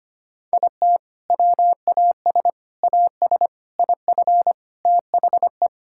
Morse code, heard as a keyed tone, about 25 wpm, 700 Hz. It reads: IT WAS AS IF THE
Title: Sentences from Top 1000 Words - Review Part 6 - 25wpm